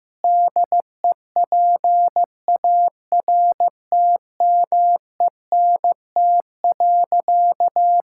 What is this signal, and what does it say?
Morse code, heard as a keyed tone, about 15 wpm, 700 Hz. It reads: DEPARTMENT.
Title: Sentences from Top 1000 Words - Review Part 5 - 15wpm